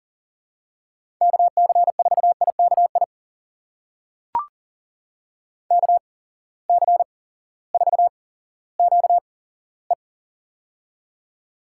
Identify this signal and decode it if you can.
Morse code, heard as a keyed tone, about 42 wpm, spaced as for 19 wpm, 700 Hz. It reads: KC4IKI KC4QE
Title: USA Call Signs - 8x Character Spacing - 40wpm